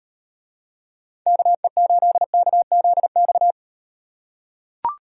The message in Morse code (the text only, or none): KE8KZX